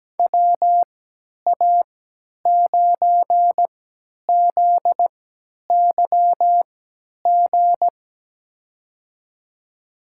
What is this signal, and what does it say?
Morse code, heard as a keyed tone, about 17 wpm, spaced as for 10 wpm, 700 Hz. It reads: WA9ZYG
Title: USA Call Signs - 6 Characters - 3x Character Spacing - 17wpm